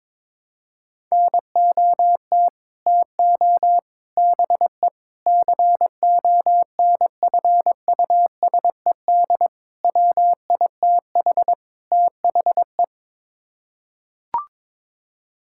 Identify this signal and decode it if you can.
Morse code, heard as a keyed tone, about 21 wpm, 700 Hz. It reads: NOT TO BE CONFUSED WITH THE
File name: practice / 22wpm / Sets of 6 Words - 22wpm